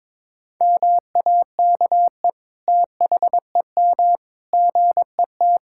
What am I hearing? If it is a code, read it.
Morse code, heard as a keyed tone, about 21 wpm, 700 Hz. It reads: MAKE THEM GET
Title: Sentences from Top 1600 Words 22wpm